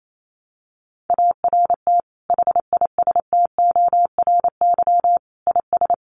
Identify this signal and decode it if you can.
Morse code, heard as a keyed tone, about 28 wpm, 700 Hz. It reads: ART HISTORY IS
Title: Sentences from Top 600 Words 28wpm